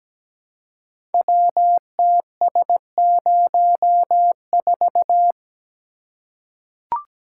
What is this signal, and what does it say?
Morse code, heard as a keyed tone, about 17 wpm, 700 Hz. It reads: WTS04
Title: Code Groups - 5 Letter:Numbers - 17wpm